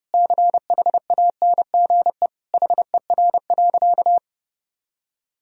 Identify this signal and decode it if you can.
Morse code, heard as a keyed tone, about 32 wpm, 700 Hz. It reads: CHANGE HER.
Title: Sentences from Top 1000 Words - Review Part 1 - 30wpm